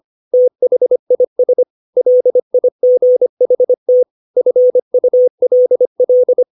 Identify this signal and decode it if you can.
Morse code, heard as a keyed tone, about 25 wpm, 500 Hz. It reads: THIS LIGHT FULL
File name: 632 words / 632 Words 25WPM